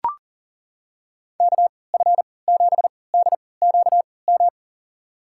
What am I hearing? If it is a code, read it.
Morse code, heard as a keyed tone, about 40 wpm, spaced as for 23 wpm, 700 Hz. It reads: KF7DQM
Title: USA Call Signs - 6 Characters - 3x Character Spacing - 40wpm